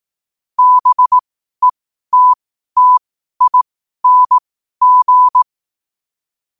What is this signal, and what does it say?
Morse code, heard as a keyed tone, about 18 wpm, spaced as for 12 wpm, 1000 Hz. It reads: BETTING
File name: qotdfiles / 12wpm